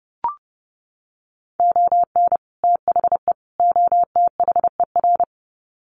Morse code, heard as a keyed tone, about 30 wpm, 700 Hz. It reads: ON THE OTHER
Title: Sets of 3 Words 30wpm